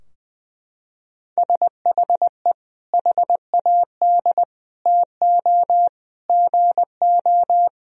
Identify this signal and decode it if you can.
Morse code, heard as a keyed tone, about 20 wpm, 700 Hz. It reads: SHE HAD TO GO
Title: Sentences from Top 1000 Words - Review Part 2 - 20wpm